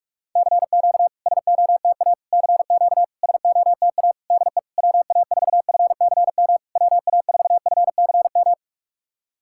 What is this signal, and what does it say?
Morse code, heard as a keyed tone, about 46 wpm, 700 Hz. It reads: CQ SOTA CQ SOTA DE WA4FCM WA4FCM
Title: SOTA QSO Practice 45wpm